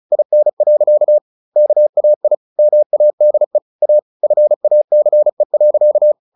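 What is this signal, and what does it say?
Morse code, heard as a keyed tone, about 34 wpm, 600 Hz. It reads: IN. KAI MADE A FACE.
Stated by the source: Morse practice at 35 wpm from CW med Gnister, sending Ken Follett's novel Never in English.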